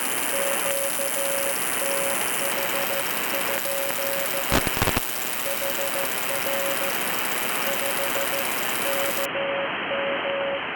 Morse code, heard as a keyed tone, about 14 wpm, 550 Hz, with noise. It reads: QTH? HR 5 KM